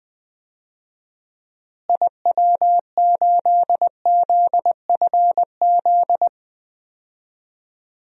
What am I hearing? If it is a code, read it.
Morse code, heard as a keyed tone, about 20 wpm, 700 Hz. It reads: IW8ZFZ
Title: DXCC-100 Call Signs - 6 Characters - Repeated 3x - 20wpm